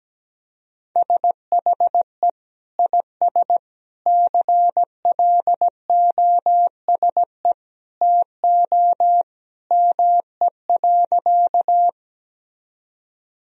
Morse code, heard as a keyed tone, about 17 wpm, 700 Hz. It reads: SHE IS CLOSE TO ME.
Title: Sentences from Top 1000 Words - Review Part 1 - 17wpm